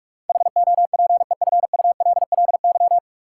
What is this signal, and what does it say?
Morse code, heard as a keyed tone, about 41 wpm, 700 Hz. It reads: HOPEFULLY
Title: Sentences from Top 1500 Words - New Word Drill Alt - 45wpm